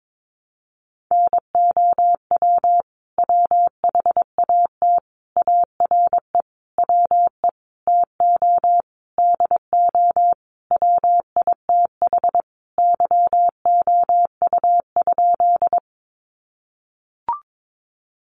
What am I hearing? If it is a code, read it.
Morse code, heard as a keyed tone, about 22 wpm, 700 Hz. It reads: NOW WHAT ARE WE TO DO WITH YOU?
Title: Sentences from Top 100 Words 22wpm